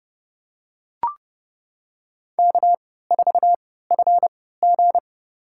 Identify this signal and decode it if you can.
Morse code, heard as a keyed tone, about 30 wpm, spaced as for 17 wpm, 700 Hz. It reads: K4FG
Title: USA Call Signs - 4 Characters - 3x Character Spacing - 30wpm